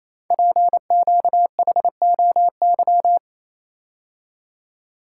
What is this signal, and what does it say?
Morse code, heard as a keyed tone, about 27 wpm, 700 Hz. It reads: PQHOY
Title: Code Groups - 5 Letters - 28wpm